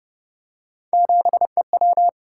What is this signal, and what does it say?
Morse code, heard as a keyed tone, about 29 wpm, 700 Hz. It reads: 7EW